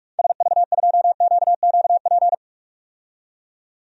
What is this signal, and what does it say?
Morse code, heard as a keyed tone, about 46 wpm, 700 Hz. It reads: SV2QQP
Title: DXCC-100 Call Signs - 6 Characters - 45wpm